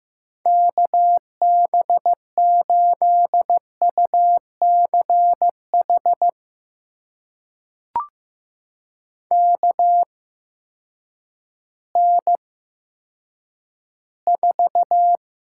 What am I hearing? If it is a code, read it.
Morse code, heard as a keyed tone, about 16 wpm, spaced as for 9 wpm, 700 Hz. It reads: KB8UCH KN4
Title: USA Call Signs - 6 Characters - 8x Character Spacing - 15wpm